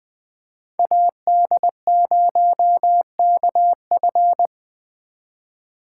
Morse code, heard as a keyed tone, about 20 wpm, 700 Hz. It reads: AD0KF